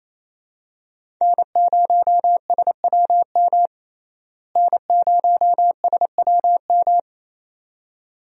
Morse code, heard as a keyed tone, about 28 wpm, 700 Hz. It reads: N0SWM N0SWM
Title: USA Call Signs - 5 Characters - Repeated 2x - 28wpm